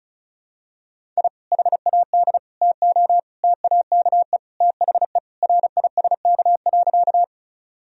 Morse code, heard as a keyed tone, about 36 wpm, 700 Hz. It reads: I HAD TO TAKE THE RISK.
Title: Sentences from Top 1000 Words - Review Part 4 - 35wpm